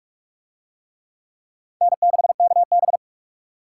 Morse code, heard as a keyed tone, about 40 wpm, 700 Hz. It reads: N6KB